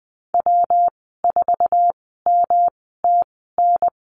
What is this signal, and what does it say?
Morse code, heard as a keyed tone, about 20 wpm, spaced as for 14 wpm, 700 Hz. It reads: W4MTN
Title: 128 Lesson 21g - Number 6 - Call Signs 20wpm